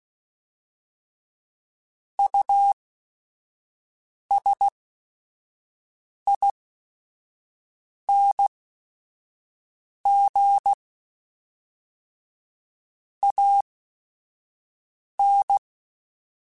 Morse code, heard as a keyed tone, about 16 wpm, spaced as for 5 wpm, 800 Hz. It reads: USING AN
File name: QSOs / QSOs-mp3s / WD0FBT 1@5